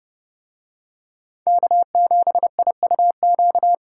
Morse code, heard as a keyed tone, about 28 wpm, 700 Hz. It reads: K7IUQ